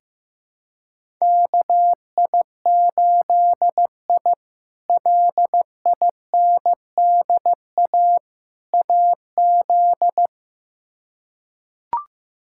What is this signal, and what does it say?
Morse code, heard as a keyed tone, about 15 wpm, 700 Hz. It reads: KI8I LINDA AZ
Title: K1USN SST QSO Practice - 15wpm